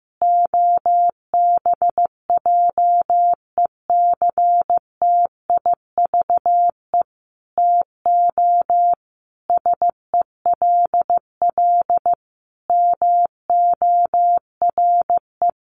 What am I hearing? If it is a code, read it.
Morse code, heard as a keyed tone, about 15 wpm, 700 Hz. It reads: OBJECTIVE TO SELL MORE
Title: Sentences from Top 1200 Words 15wpm